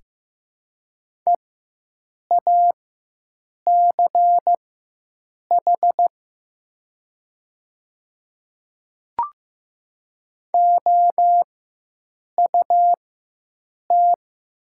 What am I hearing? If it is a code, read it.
Morse code, heard as a keyed tone, about 15 wpm, spaced as for 7 wpm, 700 Hz. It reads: EACH OUT
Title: 4x Character Spacing - Top 100 Words - 15wpm